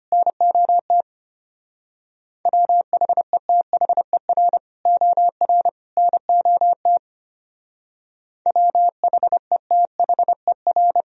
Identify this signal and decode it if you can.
Morse code, heard as a keyed tone, about 31 wpm, 700 Hz. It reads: NOT WHETHER OR NOT WHETHER